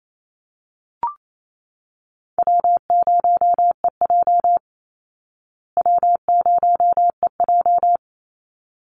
Morse code, heard as a keyed tone, about 28 wpm, 700 Hz. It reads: W0EJ W0EJ
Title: Easy Call Signs Repeated 2x - 28wpm